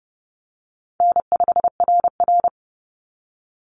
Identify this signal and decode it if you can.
Morse code, heard as a keyed tone, about 30 wpm, 700 Hz. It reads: N5RR
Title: Easy Call Signs Repeated 3x - 30wpm